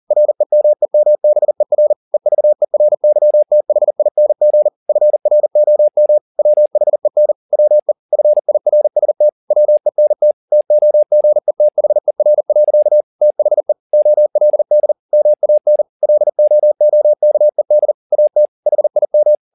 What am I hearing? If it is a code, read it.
Morse code, heard as a keyed tone, about 40 wpm, 600 Hz. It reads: REMEMBER EVERYTHING FROM WHEN WE FIRST WENT TOGETHER. THE OLD MAN LOOKED AT HIM